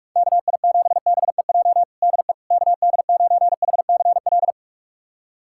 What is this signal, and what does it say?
Morse code, heard as a keyed tone, about 41 wpm, 700 Hz. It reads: KI7BEJ DE KD9HCL
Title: USA Callsign DE Callsign - 6 Characters - 45wpm